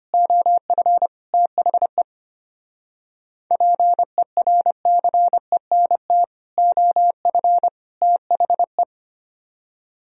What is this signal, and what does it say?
Morse code, heard as a keyed tone, about 32 wpm, 700 Hz. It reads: OF THE PERCENT OF THE